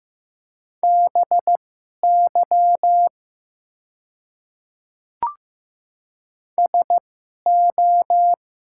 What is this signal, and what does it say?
Morse code, heard as a keyed tone, about 15 wpm, spaced as for 11 wpm, 700 Hz. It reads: BY SO
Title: 2x Character Spacing - Top 100 Words - 15wpm